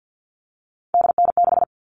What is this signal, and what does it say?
Morse code, heard as a keyed tone, about 50 wpm, 700 Hz. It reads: DN6
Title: All Cut Numbers - 3-Digit 50wpm